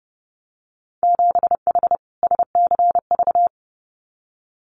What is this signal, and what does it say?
Morse code, heard as a keyed tone, about 30 wpm, 700 Hz. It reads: 7H SCV
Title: Winter Field Day Exchanges Encore 30wpm